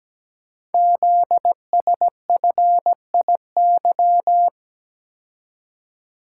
Morse code, heard as a keyed tone, about 17 wpm, 700 Hz. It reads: ZSFIY
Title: Code Groups - 5 Letters - 17wpm